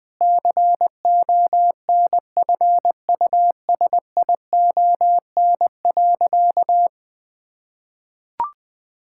Morse code, heard as a keyed tone, about 20 wpm, 700 Hz. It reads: CONFUSION.
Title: Sentences from Top 1600 Words 20wpm